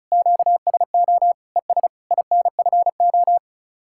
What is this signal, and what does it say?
Morse code, heard as a keyed tone, about 44 wpm, 700 Hz. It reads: QSO ES INFO